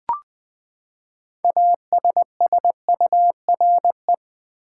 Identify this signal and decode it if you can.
Morse code, heard as a keyed tone, about 20 wpm, 700 Hz. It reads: ASSURE
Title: Six Letter Words - 20wpm